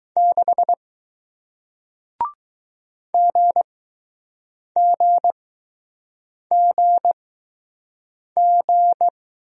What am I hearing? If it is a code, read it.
Morse code, heard as a keyed tone, about 20 wpm, spaced as for 6 wpm, 700 Hz. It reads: 6 GGGG